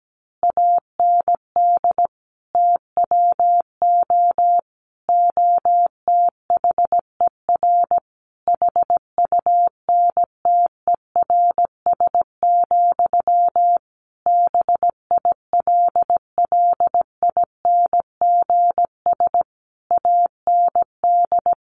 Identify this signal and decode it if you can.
Morse code, heard as a keyed tone, about 17 wpm, 700 Hz. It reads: AND TWO OTHER HUNTERS, BILLINGS AND
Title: A Sound Of Thunder 17wpm